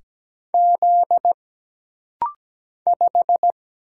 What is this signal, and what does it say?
Morse code, heard as a keyed tone, about 17 wpm, 700 Hz. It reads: Z 5
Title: Single Letter-Number Alt - Rapid-Fire 17wpm